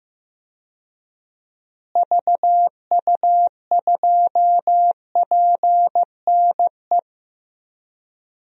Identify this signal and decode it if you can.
Morse code, heard as a keyed tone, about 15 wpm, 700 Hz. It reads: VU2PNE
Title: DXCC-100 Call Signs - 6 Characters - Repeated 2x - 15wpm